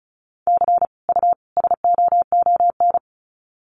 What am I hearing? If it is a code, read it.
Morse code, heard as a keyed tone, about 34 wpm, 700 Hz. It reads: C U SOON
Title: Common QSO Phrases 35wpm